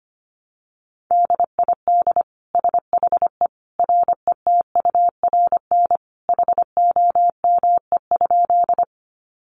Morse code, heard as a keyed tone, about 25 wpm, 700 Hz. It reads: DID SHE RETURN HOME?
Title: Sentences from Top 800 Words 25wpm